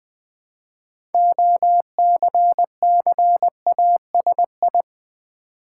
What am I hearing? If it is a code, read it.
Morse code, heard as a keyed tone, about 20 wpm, 700 Hz. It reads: OCCASI
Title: Build A Word - 3 Letters At A Time - 8-Letter Words - 20wpm